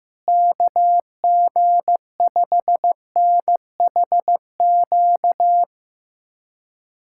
Morse code, heard as a keyed tone, about 16 wpm, 700 Hz. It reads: KG5NHQ
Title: USA Callsign DE Callsign 15wpm